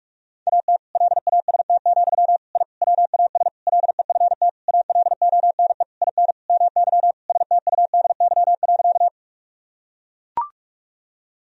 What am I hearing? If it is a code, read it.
Morse code, heard as a keyed tone, about 41 wpm, 700 Hz. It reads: AT LAST, I WAS LEFT ALONE IN MY STUDY.
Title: Sentences from Top 1000 Words - Review Part 1 - 45wpm